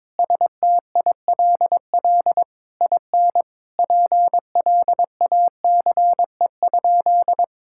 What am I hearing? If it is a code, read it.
Morse code, heard as a keyed tone, about 22 wpm, 700 Hz. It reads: STILL IN PLACE?
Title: Sentences from Top 1000 Words - Review Part 6 - 22wpm